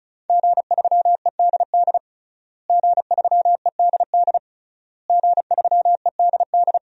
Morse code, heard as a keyed tone, about 36 wpm, 700 Hz. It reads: G3EDD G3EDD G3EDD
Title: DXCC-100 Call Signs - 5 Characters - Repeated 3x - 35wpm